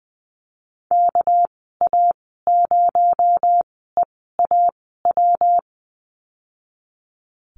Morse code, heard as a keyed tone, about 19 wpm, spaced as for 14 wpm, 700 Hz. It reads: KA0EAW
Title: 219 Lesson 35g - Number 0 - Call Signs 20wpm